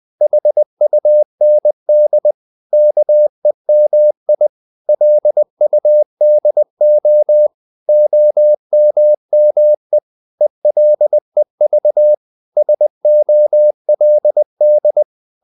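Morse code, heard as a keyed tone, about 20 wpm, 600 Hz. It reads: HUND KEMI LUDO OMME ELEV SOLD
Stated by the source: Morse code practice of short Danish words at 20 wpm, from CW med Gnister.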